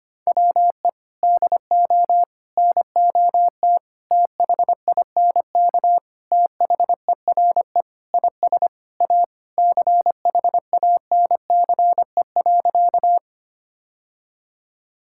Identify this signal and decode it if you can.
Morse code, heard as a keyed tone, about 25 wpm, 700 Hz. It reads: WE DO NOT THINK THERE IS A CHANCE.
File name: practice / 25wpm / Sentences with 4-Word Phrases - 25wpm